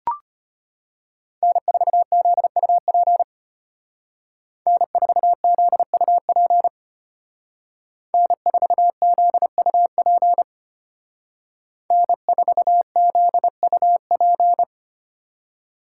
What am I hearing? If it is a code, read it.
Morse code, heard as a keyed tone, about 32 wpm, 700 Hz. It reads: N4ZUP N4ZUP N4ZUP N4ZUP